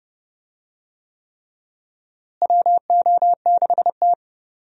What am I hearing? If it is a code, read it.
Morse code, heard as a keyed tone, about 30 wpm, 700 Hz. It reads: WO6T
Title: USA Call Signs - 4 Characters - Repeated 3x - 30wpm